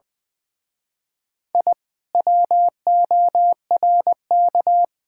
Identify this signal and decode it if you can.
Morse code, heard as a keyed tone, about 24 wpm, 700 Hz. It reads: I WORK
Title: Speed-Racing Common QSO Phrases - 20wpm